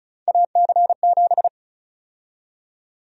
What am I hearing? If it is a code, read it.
Morse code, heard as a keyed tone, about 34 wpm, 700 Hz. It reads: AC7
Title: USA Call Sign Prefixes - 35wpm